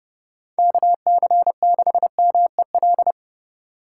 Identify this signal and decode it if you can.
Morse code, heard as a keyed tone, about 30 wpm, 700 Hz. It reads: KC6MEL
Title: USA Call Signs - 6 Characters - 3x Character Spacing - 30wpm